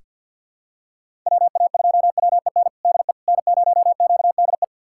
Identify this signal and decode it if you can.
Morse code, heard as a keyed tone, about 47 wpm, 700 Hz. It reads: WA2WEN DE N0QDE
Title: USA Callsign DE Callsign 50wpm